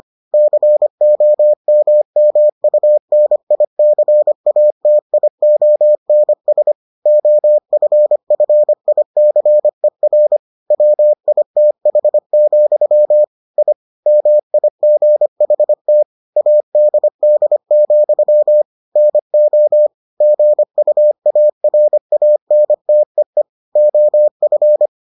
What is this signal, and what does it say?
Morse code, heard as a keyed tone, about 25 wpm, 600 Hz. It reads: COMMUNICATIONS OFFICER WITH, I MIGHT ADD, NO GUARANTEE OF